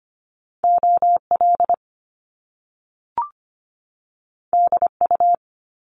Morse code, heard as a keyed tone, about 25 wpm, 700 Hz. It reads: OL DU
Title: Top Two Letter Combinations in Words 25wpm